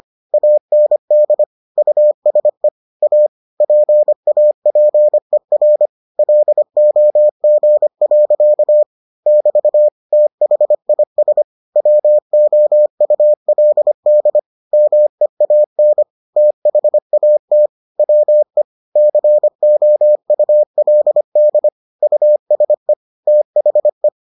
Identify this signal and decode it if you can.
Morse code, heard as a keyed tone, about 25 wpm, 600 Hz. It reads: AND USE A PAPER LOG. <BT> THIS WOULD MEAN THAT WE COULD USE THE